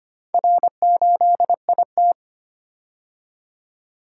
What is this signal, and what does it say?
Morse code, heard as a keyed tone, about 25 wpm, 700 Hz. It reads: R8IT
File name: practice / 25wpm / Four Letter-Number Combinations - 25wpm